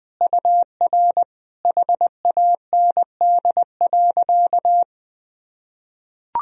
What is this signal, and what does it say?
Morse code, heard as a keyed tone, about 21 wpm, 700 Hz. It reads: UR HAND.
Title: Sentences from Top 400 Words and Abbreviations 20wpm